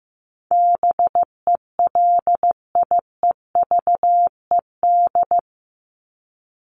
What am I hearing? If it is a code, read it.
Morse code, heard as a keyed tone, about 15 wpm, 700 Hz. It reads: BELIEVED
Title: ED Suffix - Top 100 Words with Base Words 15wpm